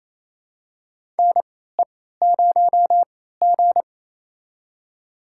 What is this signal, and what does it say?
Morse code, heard as a keyed tone, about 28 wpm, spaced as for 16 wpm, 700 Hz. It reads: NE0G
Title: USA Call Signs - 4 Characters - 3x Character Spacing - 28wpm